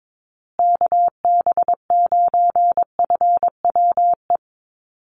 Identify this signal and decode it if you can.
Morse code, heard as a keyed tone, about 22 wpm, 700 Hz. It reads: KB9FWE
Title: USA Call Signs Repeated 2x 22wpm